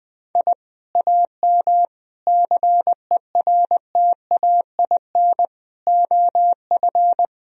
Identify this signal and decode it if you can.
Morse code, heard as a keyed tone, about 20 wpm, 700 Hz. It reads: I AM CERTAIN OF
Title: Sentences from Top 1000 Words - Review Part 2 - 20wpm